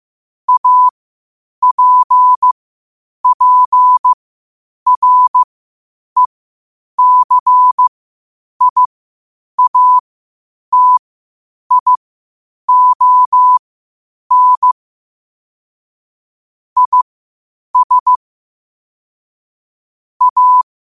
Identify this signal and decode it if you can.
Morse code, heard as a keyed tone, about 15 wpm, spaced as for 8 wpm, 1000 Hz. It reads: APPRECIATION IS A
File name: qotdfiles / 8wpm